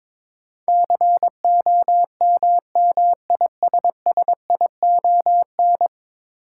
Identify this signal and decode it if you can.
Morse code, heard as a keyed tone, about 22 wpm, 700 Hz. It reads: COMMISSION
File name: practice / 22wpm / 10 Letter Words - First 3 Letters Given - 22wpm